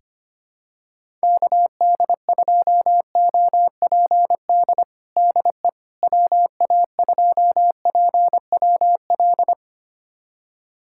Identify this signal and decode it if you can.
Morse code, heard as a keyed tone, about 25 wpm, 700 Hz. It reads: KD2OPD DE WA2PWL
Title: USA Callsign DE Callsign 25wpm